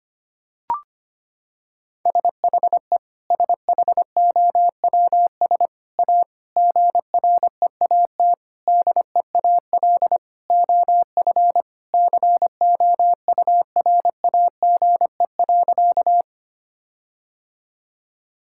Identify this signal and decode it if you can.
Morse code, heard as a keyed tone, about 25 wpm, 700 Hz. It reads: SHE SHOWS A GREAT DEAL OF COURAGE.
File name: practice / 25wpm / Sentences with 4-Word Phrases - 25wpm